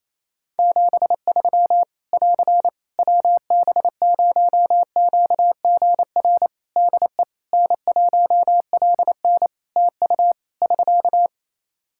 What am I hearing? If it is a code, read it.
Morse code, heard as a keyed tone, about 28 wpm, 700 Hz. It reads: 73 <AR> WB0QGR DE N1LN TU <SK>
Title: Practice QSO - Exchange 4 - Ending - 28wpm